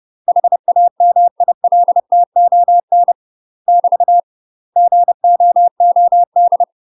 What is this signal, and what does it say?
Morse code, heard as a keyed tone, about 30 wpm, 700 Hz. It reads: HAMILTON <BT> GOOD
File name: UK MASTER QSOs TRAINING / 30WPM / UK MASTER QSOs 0030